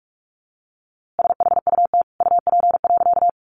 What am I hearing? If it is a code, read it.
Morse code, heard as a keyed tone, about 44 wpm, 700 Hz. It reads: SHUT UP.
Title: A Sound Of Thunder 45wpm